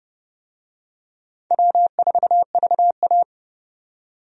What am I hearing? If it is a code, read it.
Morse code, heard as a keyed tone, about 30 wpm, 700 Hz. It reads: W4VA
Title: USA Call Signs - 4 Characters - 30wpm